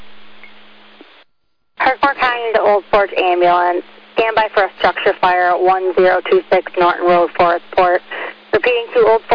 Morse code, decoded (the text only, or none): none